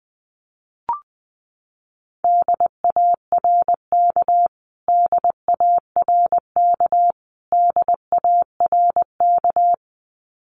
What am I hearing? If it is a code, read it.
Morse code, heard as a keyed tone, about 20 wpm, 700 Hz. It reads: DARK DARK DARK
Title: Top 500 Words Repeated 20wpm